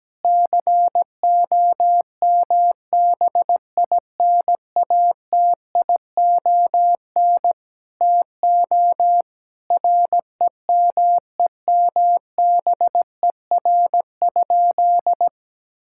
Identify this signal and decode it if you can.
Morse code, heard as a keyed tone, about 17 wpm, 700 Hz. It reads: COMBINATION TO REMEMBER?
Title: Sentences from Top 1000 Words - Review Part 6 - 17wpm